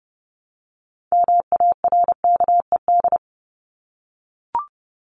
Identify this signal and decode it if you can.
Morse code, heard as a keyed tone, about 28 wpm, 700 Hz. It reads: MARKED
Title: ED Suffix - Top 100 Words with Base Words 30wpm